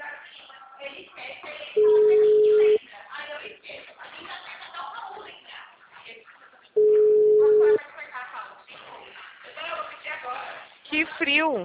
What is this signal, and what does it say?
Ringback tone (Europe, 425 Hz), with speech.